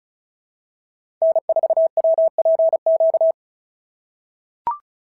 Morse code, heard as a keyed tone, about 36 wpm, 690 Hz. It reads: N4WPQ
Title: SOTA QSO Practice 35wpm